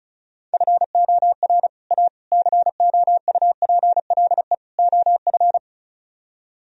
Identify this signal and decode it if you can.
Morse code, heard as a keyed tone, about 44 wpm, 700 Hz. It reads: FOR A COUPLE OF